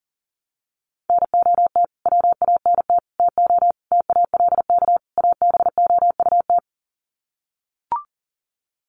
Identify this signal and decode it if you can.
Morse code, heard as a keyed tone, about 40 wpm, 700 Hz. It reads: NOT WANT TO TALK ABOUT
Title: Sets of 5 Words 40wpm